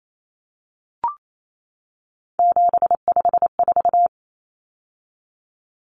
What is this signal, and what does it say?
Morse code, heard as a keyed tone, about 28 wpm, 700 Hz. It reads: 754
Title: Common Cut Numbers - 3-Digit 28wpm